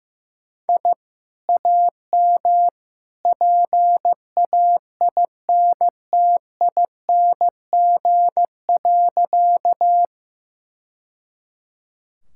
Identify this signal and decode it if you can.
Morse code, heard as a keyed tone, about 16 wpm, 700 Hz. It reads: I AM PAINTING.
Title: Sentences from Top 1000 Words - Review Part 6 - 15wpm